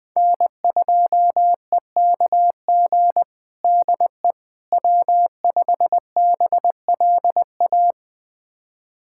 Morse code, heard as a keyed tone, about 21 wpm, 700 Hz. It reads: N2EKG DE W5BLA
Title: USA Callsign DE Callsign - 5 Characters - 20wpm